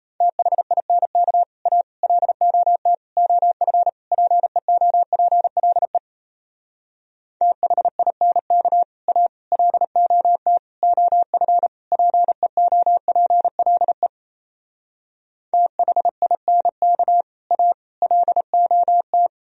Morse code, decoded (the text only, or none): THINK A LOT OF PEOPLE THINK A LOT OF PEOPLE THINK A LOT